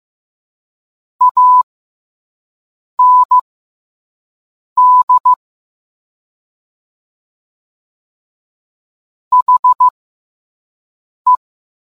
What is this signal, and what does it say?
Morse code, heard as a keyed tone, about 15 wpm, spaced as for 5 wpm, 1000 Hz. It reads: AND HE